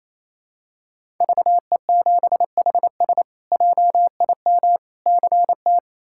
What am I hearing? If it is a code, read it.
Morse code, heard as a keyed tone, about 28 wpm, 700 Hz. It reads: VE7HS JIM CT
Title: K1USN SST QSO Practice - 28wpm